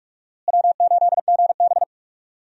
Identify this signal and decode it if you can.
Morse code, heard as a keyed tone, about 45 wpm, 700 Hz. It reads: W8GB
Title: USA Call Signs - 4 Characters - Repeated 2x - 45wpm